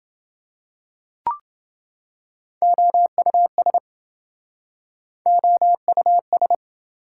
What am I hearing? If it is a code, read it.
Morse code, heard as a keyed tone, about 27 wpm, 700 Hz. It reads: OUS OUS